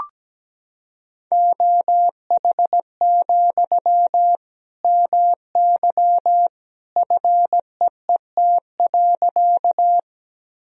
Morse code, heard as a keyed tone, about 17 wpm, 700 Hz. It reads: OH, MY FEET.
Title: Sentences from Top 1000 Words - Review Part 1 - 17wpm